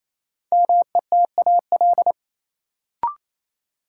Morse code, heard as a keyed tone, about 28 wpm, 700 Hz. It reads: METAL
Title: Sentences from Top 1000 Words - Review Part 6 - New Word Drill - 28wpm